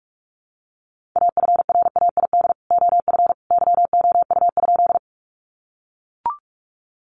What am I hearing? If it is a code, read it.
Morse code, heard as a keyed tone, about 44 wpm, 700 Hz. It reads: AFRAID OF YOU?
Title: Sentences from Top 1500 Words 45wpm